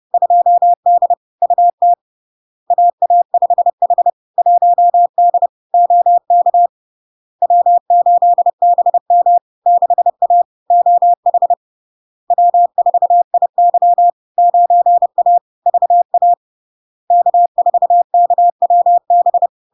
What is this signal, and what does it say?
Morse code, heard as a keyed tone, about 30 wpm, 700 Hz. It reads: 2D UT AA5H 1D OK W8BM 6A OH W4IY 9A VA K4KWB
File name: LARK-FD-2022-Log-in-CW-30wpm-700-Hz